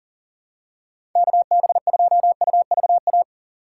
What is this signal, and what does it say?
Morse code, heard as a keyed tone, about 40 wpm, 700 Hz. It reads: KB2UVA
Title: USA Call Signs - 6 Characters - 3x Character Spacing - 40wpm